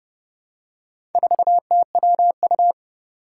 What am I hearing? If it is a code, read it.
Morse code, heard as a keyed tone, about 30 wpm, 700 Hz. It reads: 4TWU
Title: Four Challenging Letter-Number Combinations 30wpm